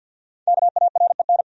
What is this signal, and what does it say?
Morse code, heard as a keyed tone, about 47 wpm, 700 Hz. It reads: KAREN
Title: USA Common Names - 50wpm